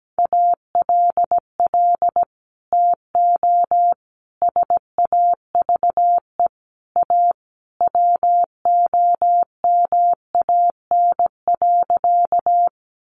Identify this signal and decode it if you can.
Morse code, heard as a keyed tone, about 17 wpm, 700 Hz. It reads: ALL TO SAVE A WOMAN.
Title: Sentences from Top 1100 Words 17wpm